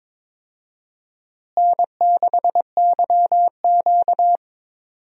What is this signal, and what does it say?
Morse code, heard as a keyed tone, about 22 wpm, 700 Hz. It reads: N6YQ
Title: USA Call Signs - 4 Characters - Repeated 3x - 22wpm